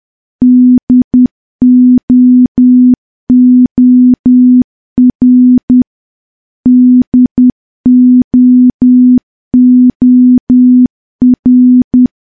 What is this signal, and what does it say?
Morse code, heard as a keyed tone, about 10 wpm, 250 Hz. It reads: DOOR DOOR
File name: morse / morse-training-10WPM